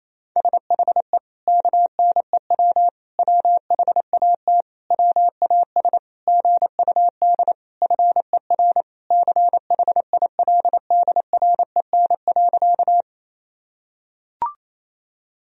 Morse code, heard as a keyed tone, about 28 wpm, 700 Hz. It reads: SHE KNEW WHAT WAS GUD FER CHILDREN.
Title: Sentences from Top 400 Words and Abbreviations 28wpm